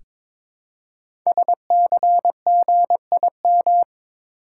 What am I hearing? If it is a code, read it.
Morse code, heard as a keyed tone, about 22 wpm, 700 Hz. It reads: SCGIM
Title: Code Groups - 5 Letter:Numbers - 22wpm